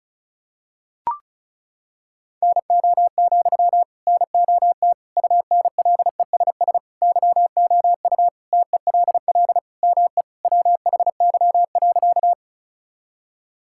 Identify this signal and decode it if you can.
Morse code, heard as a keyed tone, about 34 wpm, 690 Hz. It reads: NO, NOT UNLESS YOU TELL ME WHY.
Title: Sentences from Top 1000 Words - Review Part 6 - 35wpm